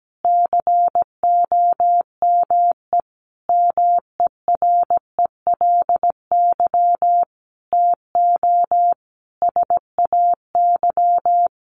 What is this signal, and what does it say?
Morse code, heard as a keyed tone, about 17 wpm, 700 Hz. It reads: COME MERELY TO SAY
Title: Sentences from Top 1400 Words 17wpm